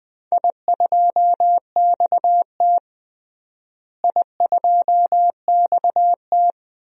Morse code, heard as a keyed tone, about 20 wpm, 700 Hz. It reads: I2XT I2XT
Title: DXCC-100 Call Signs - 4 Characters - Repeated 2x - 20wpm